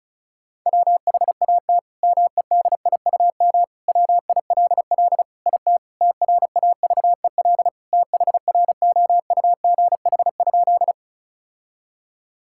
Morse code, heard as a keyed tone, about 36 wpm, 700 Hz. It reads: WHAT MEDIUM WILL IT TRAVEL THROUGH?
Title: Sentences from Top 1000 Words - Review Part 6 - 35wpm